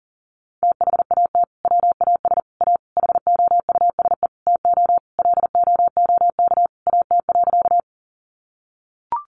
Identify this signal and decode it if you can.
Morse code, heard as a keyed tone, about 40 wpm, 700 Hz. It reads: THAT WAS A HOUSE TO LOOK AT.
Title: Sentences from Top 200 Words 40wpm